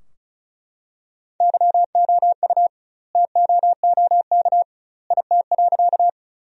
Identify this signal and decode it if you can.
Morse code, heard as a keyed tone, about 36 wpm, 700 Hz. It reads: YOU TOOK IT.
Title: 2x Word-Spacing - Sentences from Top 300 Words - 35wpm